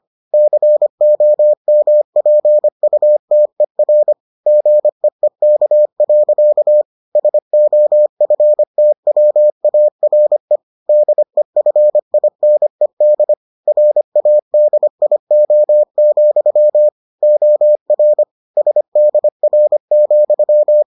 Morse code, heard as a keyed tone, about 25 wpm, 600 Hz. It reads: COMPUTER GEEK. SOFTWARE DEFINED RADIO, OR SDR,